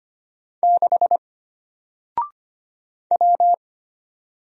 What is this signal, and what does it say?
Morse code, heard as a keyed tone, about 25 wpm, 700 Hz. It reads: 6 W
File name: Single-Letter-Number-25wpm